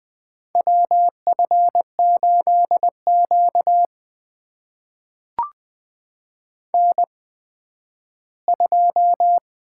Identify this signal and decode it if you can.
Morse code, heard as a keyed tone, about 20 wpm, spaced as for 12 wpm, 700 Hz. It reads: WF8Q N2
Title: USA Call Signs - 8x Character Spacing - 20wpm